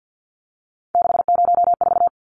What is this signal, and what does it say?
Morse code, heard as a keyed tone, about 50 wpm, 700 Hz. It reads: 604
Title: Numbers - 3-Digit 50wpm